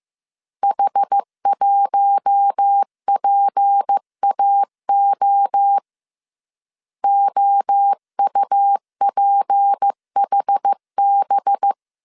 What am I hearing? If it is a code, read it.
Morse code, heard as a keyed tone, about 15 wpm, 800 Hz. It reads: H1PAO OUPHB